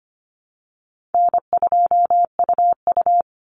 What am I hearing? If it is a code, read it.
Morse code, heard as a keyed tone, about 25 wpm, 700 Hz. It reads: N2UU
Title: Call Signs Repeated 2x - 25wpm